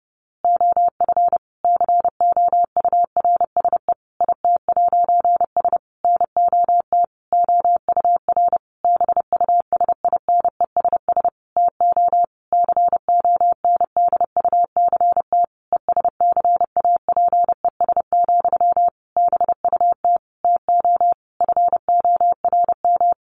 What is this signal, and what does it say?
Morse code, heard as a keyed tone, about 28 wpm, 700 Hz. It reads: OF COURSE IT'S NOT OUR BUSINESS TO CONDUCT ESCAPES, BUT TO FORM
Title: A Sound Of Thunder 30wpm